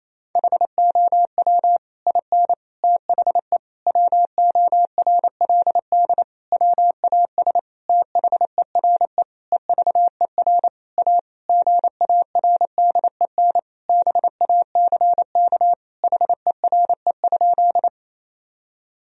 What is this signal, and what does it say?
Morse code, heard as a keyed tone, about 28 wpm, 700 Hz. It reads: HOW IN THE WORLD WAS THERE EVER A GARDEN BACK HERE?
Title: Sentences from Top 1000 Words - Review Part 4 - 28wpm